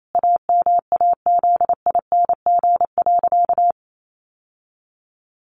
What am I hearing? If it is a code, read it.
Morse code, heard as a keyed tone, about 28 wpm, 700 Hz. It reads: AMAZING.
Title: Sentences from Top 1500 Words 28wpm